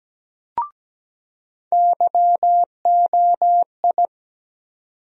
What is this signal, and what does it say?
Morse code, heard as a keyed tone, about 17 wpm, 700 Hz. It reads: YOI
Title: USA Call Sign Postfixes - 3 Letters 17wpm